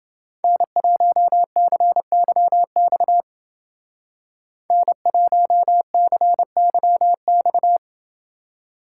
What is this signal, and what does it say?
Morse code, heard as a keyed tone, about 29 wpm, 700 Hz. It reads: N1CYX N1CYX